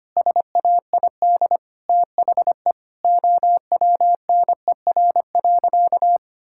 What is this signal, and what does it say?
Morse code, heard as a keyed tone, about 25 wpm, 700 Hz. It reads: SAID THE OWNER.
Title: Sentences from Top 1600 Words 25wpm